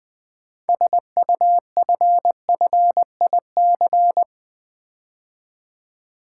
Morse code, heard as a keyed tone, about 20 wpm, 700 Hz. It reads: SUFFIC
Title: Build A Word - 6 Letters At A Time - 12-Letter Words - 20wpm